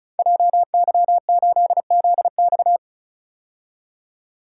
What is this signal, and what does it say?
Morse code, heard as a keyed tone, about 36 wpm, 700 Hz. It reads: JY8ZX
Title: Code Groups - 5 Letter:Numbers - 35wpm